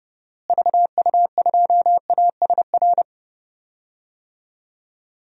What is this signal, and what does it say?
Morse code, heard as a keyed tone, about 32 wpm, 700 Hz. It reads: VU2ASR
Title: DXCC-100 Call Signs - 6 Characters - Repeated 2x - 30wpm